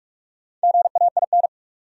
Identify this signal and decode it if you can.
Morse code, heard as a keyed tone, about 47 wpm, 700 Hz. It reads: GAIN
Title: Sentences from Top 1000 Words - New Word Drill - 45wpm